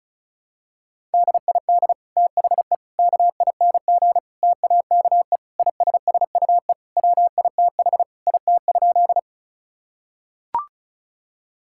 Morse code, heard as a keyed tone, about 34 wpm, 700 Hz. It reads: DID THE KING TAKE ISSUE WITH IT?
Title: Sentences from Top 1000 Words - Review Part 6 - 35wpm